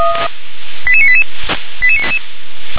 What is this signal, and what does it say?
Five-tone selective call (ZVEI), heard as quick runs of five tones, tones of about 70 ms. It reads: 88988
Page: ZVEI